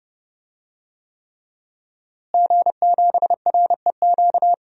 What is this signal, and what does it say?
Morse code, heard as a keyed tone, about 32 wpm, 700 Hz. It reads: G7REQ
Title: DXCC-100 Call Signs - 5 Characters - Repeated 3x - 30wpm